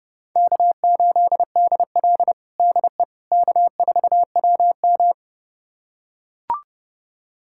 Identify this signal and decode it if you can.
Morse code, heard as a keyed tone, about 30 wpm, 700 Hz. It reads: K8DL DE K4WM
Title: USA Callsign DE Callsign - 4 Characters - 30wpm